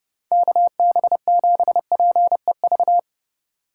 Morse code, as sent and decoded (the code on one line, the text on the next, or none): -.- -... --... .--. ...-
KB7PEV